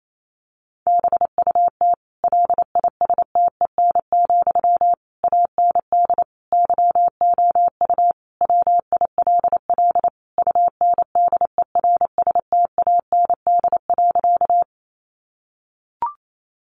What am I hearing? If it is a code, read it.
Morse code, heard as a keyed tone, about 28 wpm, 700 Hz. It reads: BUT LISTEN, AND YOU WILL UNDERSTAND.
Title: Sentences from Top 600 Words 28wpm